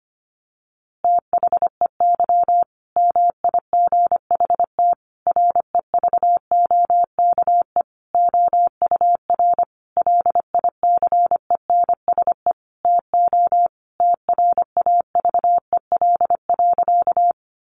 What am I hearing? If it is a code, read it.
Morse code, heard as a keyed tone, about 25 wpm, 700 Hz. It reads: THEY MIGHT REVOKE OUR LICENSE TO TRAVEL.
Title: A Sound Of Thunder 25wpm